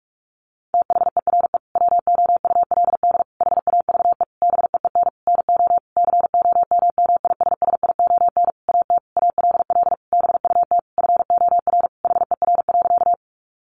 Morse code, heard as a keyed tone, about 44 wpm, 700 Hz. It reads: THERE WOULD HAVE BEEN NO COMMISSION AT ALL BUT FOR HER.
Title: Sentences from Top 1400 Words 45wpm